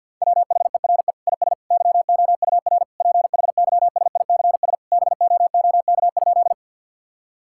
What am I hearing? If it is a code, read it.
Morse code, heard as a keyed tone, about 50 wpm, 700 Hz. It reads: WHERE IS YOUR PHYSICS BOOK?
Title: Sentences from Top 1000 Words - Review Part 5 - 50wpm